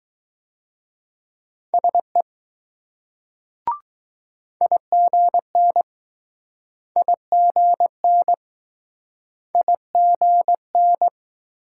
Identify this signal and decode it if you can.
Morse code, heard as a keyed tone, about 20 wpm, 700 Hz. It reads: SE IGN IGN IGN